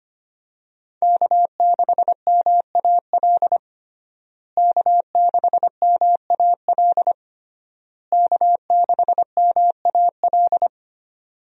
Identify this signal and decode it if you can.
Morse code, heard as a keyed tone, about 25 wpm, 700 Hz. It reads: K6MAL K6MAL K6MAL
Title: USA Call Signs - 5 Characters - Repeated 3x - 25wpm